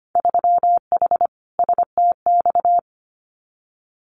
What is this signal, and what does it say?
Morse code, heard as a keyed tone, about 25 wpm, 700 Hz. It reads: 3H STX
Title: Winter Field Day Exchanges Encore 25wpm